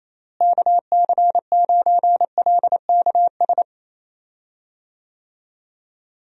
Morse code, heard as a keyed tone, about 28 wpm, 700 Hz. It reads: KC9LKS
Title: USA Call Signs - 6 Characters - Repeated 3x - 28wpm